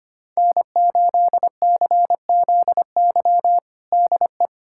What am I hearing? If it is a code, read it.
Morse code, heard as a keyed tone, about 25 wpm, 700 Hz. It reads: N8CZY DE
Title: USA Callsign DE Callsign - 5 Characters - 25wpm